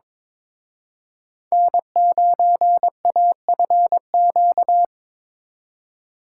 Morse code, heard as a keyed tone, about 26 wpm, 700 Hz. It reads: N9AFQ
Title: Speed Racing - USA 1x3 Call Signs 22wpm